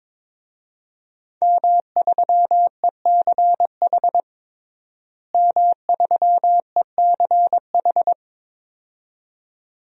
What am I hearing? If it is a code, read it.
Morse code, heard as a keyed tone, about 22 wpm, 700 Hz. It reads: M3ECH M3ECH
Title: DXCC-100 Call Signs - 5 Characters - Repeated 2x - 22wpm